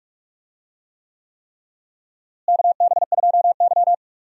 Morse code, heard as a keyed tone, about 44 wpm, 700 Hz. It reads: KB2Y
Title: USA Call Signs - 4 Characters - Repeated 3x - 45wpm